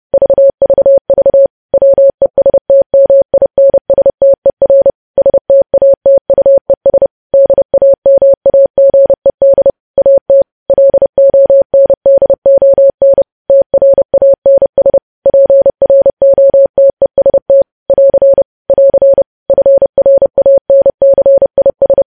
Morse code, heard as a keyed tone, about 30 wpm, 550 Hz. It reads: VVV WESTMINSTER STATUES DAMAGED AT LONDON TRANS PROTEST <AR> <AR> FRANCIS
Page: News Headlines in Morse Code at 30 WPM